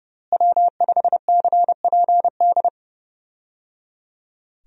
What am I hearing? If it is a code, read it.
Morse code, heard as a keyed tone, about 30 wpm, 700 Hz. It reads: W5CPD
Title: USA Call Signs - 5 Characters - Repeated 3x - 30wpm